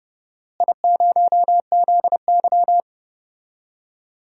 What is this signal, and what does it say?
Morse code, heard as a keyed tone, about 30 wpm, 700 Hz. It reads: I0ZY
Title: DXCC-100 Call Signs - 4 Characters - Repeated 3x - 30wpm